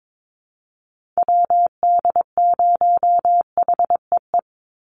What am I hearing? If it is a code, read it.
Morse code, heard as a keyed tone, about 22 wpm, 700 Hz. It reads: WD0HEE
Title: USA Call Signs 22wpm